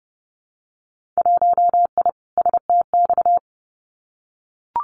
Morse code, heard as a keyed tone, about 30 wpm, 700 Hz. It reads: 1I STX
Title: Winter Field Day Exchanges 30wpm